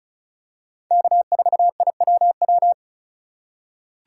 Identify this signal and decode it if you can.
Morse code, heard as a keyed tone, about 36 wpm, 700 Hz. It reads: K4IWW
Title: USA Call Signs - 5 Characters - 4x Character Spacing - 35wpm